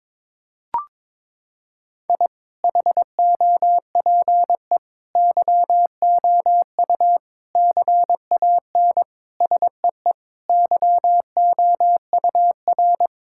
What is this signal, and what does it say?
Morse code, heard as a keyed tone, about 22 wpm, 700 Hz. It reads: I HOPE YOU CAN SEE YOUR
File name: practice / 22wpm / Sentences with 8-Word Phrases - 22wpm